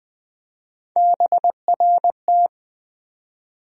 Morse code, heard as a keyed tone, about 20 wpm, 700 Hz. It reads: BRT
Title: USA Call Sign Postfixes - 3 Letters 20wpm